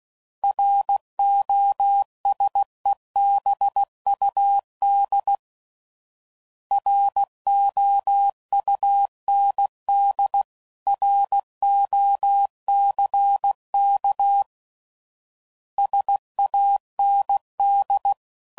Morse code, heard as a keyed tone, about 16 wpm, 800 Hz. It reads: ROSEBUD ROUND ROCK SAND
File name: CW Code Groups @15WPM / fromSM's / Cities